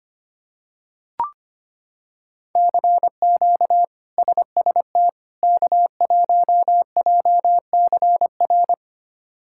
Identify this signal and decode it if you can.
Morse code, heard as a keyed tone, about 25 wpm, 700 Hz. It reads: CQ SST K1JCR
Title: K1USN SST QSO Practice - 25wpm